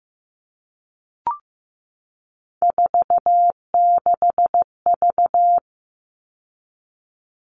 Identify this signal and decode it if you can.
Morse code, heard as a keyed tone, about 15 wpm, 700 Hz. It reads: 46V
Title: All Cut Numbers - 3-Digit 15wpm